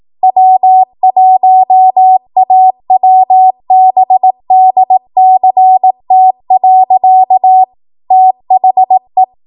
Morse code, heard as a keyed tone, about 18 wpm, 750 Hz. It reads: W1AWBDCT. THE
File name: Archive / 18 WPM / 18WPM